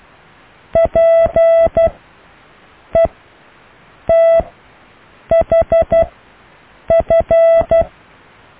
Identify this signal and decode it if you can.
Morse code, heard as a keyed tone, about 12 wpm, spaced as for 6 wpm, 660 Hz. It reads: PETHF
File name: cw / EKTPV+BFHNW